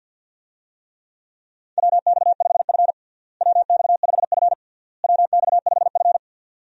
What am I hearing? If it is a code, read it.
Morse code, heard as a keyed tone, about 51 wpm, 700 Hz. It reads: WX5F WX5F WX5F
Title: USA Call Signs - 4 Characters - Repeated 3x - 50wpm